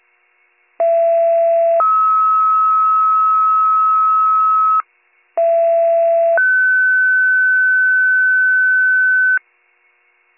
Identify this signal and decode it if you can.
Two-tone page, heard as a one-second tone, then a three-second tone, about 670 then 1290 Hz, about 670 then 1500 Hz. It is heard from 0.8 to 9.4 s.